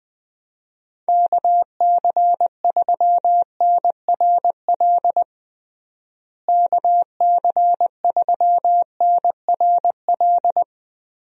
Morse code, heard as a keyed tone, about 20 wpm, 700 Hz. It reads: KC3NRL KC3NRL
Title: USA Call Signs - 6 Characters - Repeated 3x - 20wpm